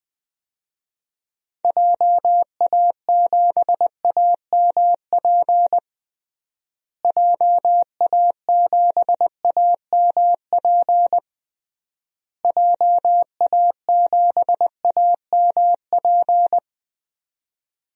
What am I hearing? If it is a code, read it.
Morse code, heard as a keyed tone, about 20 wpm, 700 Hz. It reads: JA7AMP JA7AMP JA7AMP
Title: DXCC-100 Call Signs - 6 Characters - Repeated 3x - 20wpm